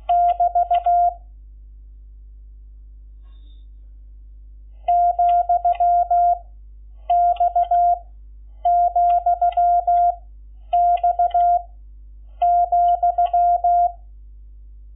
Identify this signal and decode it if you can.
Morse code, heard as a keyed tone, about 16 wpm, 700 Hz. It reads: <BT> , X , X ,
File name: Lecciones / CLASE Nº11